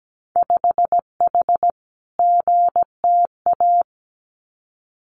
Morse code, heard as a keyed tone, about 17 wpm, 700 Hz. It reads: 5H GTA
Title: Winter Field Day Exchanges Encore 17wpm